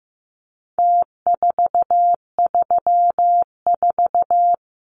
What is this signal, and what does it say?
Morse code, heard as a keyed tone, about 15 wpm, 700 Hz. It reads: T434
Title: Numbers - Contest Style - 4-Digit 15wpm